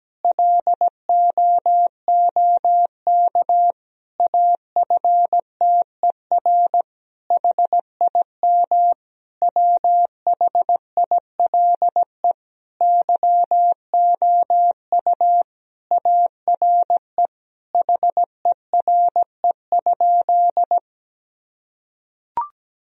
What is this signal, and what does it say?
Morse code, heard as a keyed tone, about 17 wpm, 700 Hz. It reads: LOOK AFTER HIM WHILE YOU ARE HERE?
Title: Sentences from Top 1000 Words - Review Part 1 - 17wpm